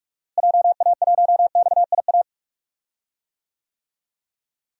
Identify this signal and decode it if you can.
Morse code, heard as a keyed tone, about 45 wpm, 700 Hz. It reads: JA1XIA
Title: DXCC-100 Call Signs - 6 Characters - Repeated 2x - 45wpm